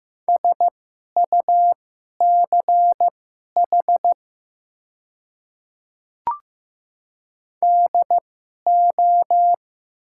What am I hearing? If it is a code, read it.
Morse code, heard as a keyed tone, about 15 wpm, spaced as for 11 wpm, 700 Hz. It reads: SUCH DO
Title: 2x Character Spacing - Top 200 Words - 15wpm